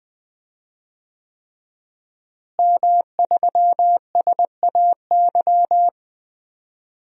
Morse code, heard as a keyed tone, about 20 wpm, 700 Hz. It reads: M3SAY
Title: DXCC-100 Call Signs - 5 Characters - Repeated 3x - 20wpm